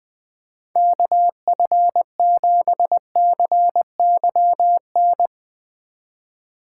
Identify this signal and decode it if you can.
Morse code, heard as a keyed tone, about 21 wpm, 700 Hz. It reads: KF7CYN
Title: USA Call Signs - 6 Characters - 3x Character Spacing - 20wpm